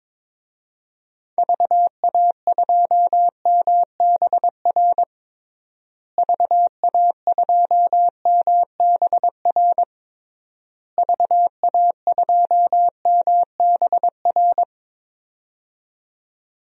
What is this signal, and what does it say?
Morse code, heard as a keyed tone, about 22 wpm, 700 Hz. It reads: VA2MBR VA2MBR VA2MBR
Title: DXCC-100 Call Signs - 6 Characters - Repeated 3x - 22wpm